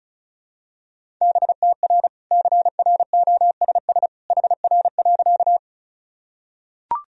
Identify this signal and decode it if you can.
Morse code, heard as a keyed tone, about 36 wpm, 700 Hz. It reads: BTR CROSS HR.
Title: Sentences from Top 400 Words and Abbreviations 35wpm